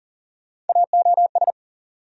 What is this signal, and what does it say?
Morse code, heard as a keyed tone, about 40 wpm, 700 Hz. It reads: AOS
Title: USA Call Sign Postfixes - 3 Letters 40wpm